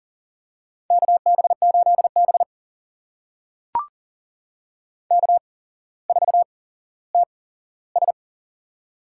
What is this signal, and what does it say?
Morse code, heard as a keyed tone, about 40 wpm, spaced as for 17 wpm, 700 Hz. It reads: KB8B K4TS
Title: USA Call Signs - 8x Character Spacing - 40wpm